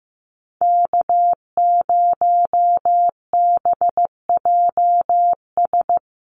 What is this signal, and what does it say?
Morse code, heard as a keyed tone, about 15 wpm, 700 Hz. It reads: K0BJS
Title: USA Call Signs 15wpm